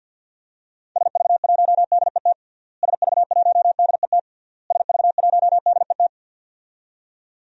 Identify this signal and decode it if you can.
Morse code, heard as a keyed tone, about 48 wpm, 700 Hz. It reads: SV1DET SV1DET SV1DET
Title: DXCC-100 Call Signs - 6 Characters - Repeated 3x - 50wpm